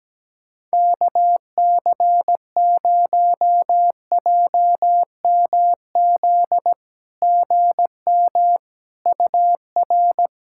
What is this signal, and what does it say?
Morse code, heard as a keyed tone, about 17 wpm, 700 Hz. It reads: KC0JMZ GM UR
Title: SOTA QSO Practice 17wpm